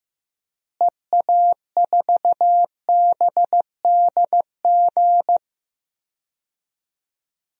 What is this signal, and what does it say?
Morse code, heard as a keyed tone, about 15 wpm, 700 Hz. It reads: EA4BDG
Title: DXCC-100 Call Signs - 6 Characters - Repeated 3x - 15wpm